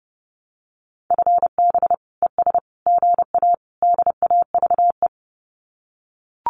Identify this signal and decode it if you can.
Morse code, heard as a keyed tone, about 30 wpm, 700 Hz. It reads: FB ES GA DAVE
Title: Common QSO Phrases Encore 30wpm